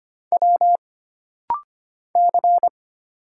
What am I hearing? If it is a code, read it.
Morse code, heard as a keyed tone, about 25 wpm, 700 Hz. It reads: W C
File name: Single Character (Letter:Number:Punctuation:Prosign) - Rapid-Fire - 25wpm